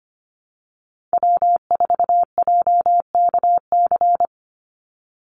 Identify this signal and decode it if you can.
Morse code, heard as a keyed tone, about 25 wpm, 700 Hz. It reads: W4JKC
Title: NA SOTA Chaser Callsigns 25wpm